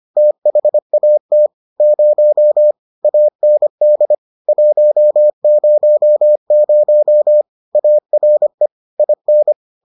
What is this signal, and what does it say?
Morse code, heard as a keyed tone, about 25 wpm, 600 Hz. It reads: THAT 0 AND 100 ARE IN